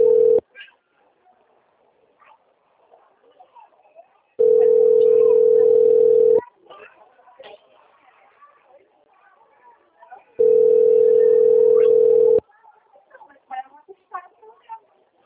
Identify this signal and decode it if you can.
Ringback tone (North America), 440 + 480 Hz, with speech.
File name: out- - - - - .9729